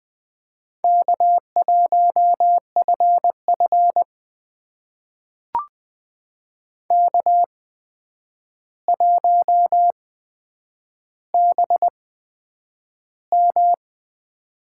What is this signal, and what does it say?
Morse code, heard as a keyed tone, about 20 wpm, spaced as for 9 wpm, 700 Hz. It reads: K1FF K1BM
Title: USA Call Signs - 4 Characters - 8x Character Spacing - 20wpm